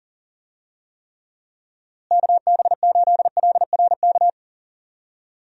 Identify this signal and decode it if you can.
Morse code, heard as a keyed tone, about 40 wpm, 700 Hz. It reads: KB8LRK
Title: USA Call Signs - 6 Characters - Repeated 3x - 40wpm